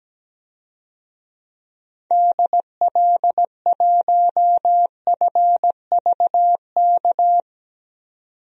Morse code, heard as a keyed tone, about 17 wpm, 700 Hz. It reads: DL1FVK
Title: DXCC-100 Call Signs - 6 Characters - 17wpm